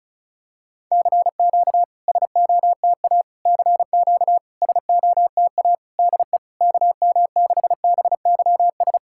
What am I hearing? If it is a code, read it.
Morse code, heard as a keyed tone, about 36 wpm, 690 Hz. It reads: CQ SOTA CQ SOTA DE KM6BYS
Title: SOTA QSO Practice 35wpm